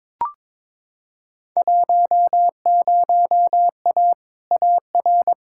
Morse code, heard as a keyed tone, about 22 wpm, 700 Hz. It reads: 10A AR
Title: Field Day Exchanges Encore - 22wpm